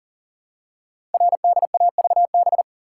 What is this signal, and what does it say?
Morse code, heard as a keyed tone, about 42 wpm, 700 Hz. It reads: RDAVB
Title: Code Groups - 5 Letters - 40wpm